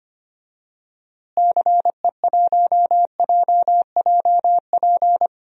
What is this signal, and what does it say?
Morse code, heard as a keyed tone, about 25 wpm, 700 Hz. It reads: CE1JJP
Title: DXCC-100 Call Signs - 6 Characters - Repeated 2x - 25wpm